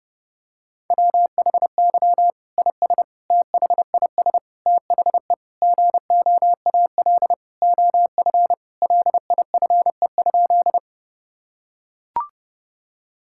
Morse code, heard as a keyed tone, about 30 wpm, 700 Hz. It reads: WHY IS THIS THE GOAL OF LIFE?
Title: Sentences from Top 1000 Words - Review Part 5 - 30wpm